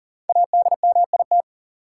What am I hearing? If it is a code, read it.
Morse code, heard as a keyed tone, about 40 wpm, 700 Hz. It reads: ADMIT
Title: Five Letter Words - 40wpm